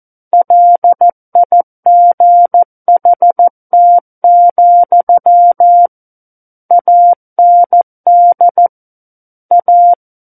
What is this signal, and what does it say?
Morse code, heard as a keyed tone, about 14 wpm, 700 Hz. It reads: LIGHT, AND A
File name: Book One / 05-Chapter 5 - 14 WPM